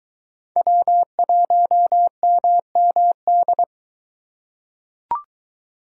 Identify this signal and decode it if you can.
Morse code, heard as a keyed tone, about 22 wpm, 700 Hz. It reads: W1MMD